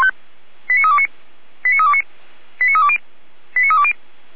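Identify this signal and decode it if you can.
Five-tone selective call (ZVEI), heard as quick runs of five tones, tones of about 70 ms. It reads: 89329 89329 89320 89320
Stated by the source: ZVEI five-tone selective call 89504 received on SDR.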